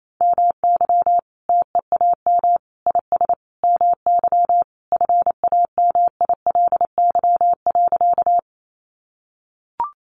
Morse code, heard as a keyed tone, about 28 wpm, 700 Hz. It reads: MY TEAM IS MY FAMILY.
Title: Sentences from Top 1500 Words 28wpm